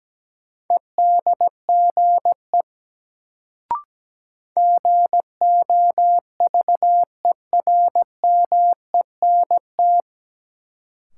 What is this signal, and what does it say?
Morse code, heard as a keyed tone, about 17 wpm, 700 Hz. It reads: EDGE GOVERMENT
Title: Sentences from Top 1300 Words - New Word Drill - 17wpm